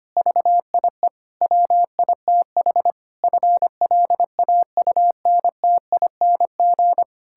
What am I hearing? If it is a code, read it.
Morse code, heard as a keyed tone, about 25 wpm, 700 Hz. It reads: VIE WITH FLAUNTING
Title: Pangram Challenge v2 - 25wpm